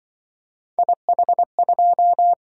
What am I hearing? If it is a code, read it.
Morse code, heard as a keyed tone, about 24 wpm, 700 Hz. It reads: IH2
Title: Speed-Racing - Three Letter-Number Combinations - 20wpm